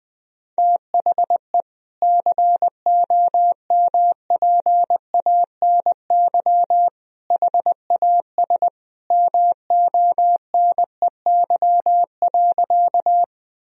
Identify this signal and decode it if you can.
Morse code, heard as a keyed tone, about 20 wpm, 700 Hz. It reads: THE COMPANY HAS MONEY.
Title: Sentences from Top 1000 Words - Review Part 3 - 20wpm